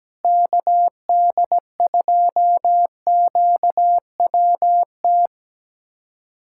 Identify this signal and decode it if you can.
Morse code, heard as a keyed tone, about 17 wpm, 700 Hz. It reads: KD2QWT
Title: USA Callsign DE Callsign - 6 Characters - 17wpm